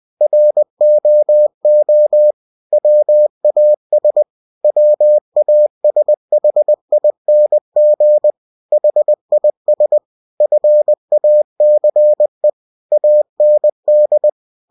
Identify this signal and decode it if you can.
Morse code, heard as a keyed tone, about 20 wpm, 600 Hz. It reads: ROO WAS WASHING HIS FACE AND
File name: cw-books / winnie-the-pooh / winnie-20wpm-page-0085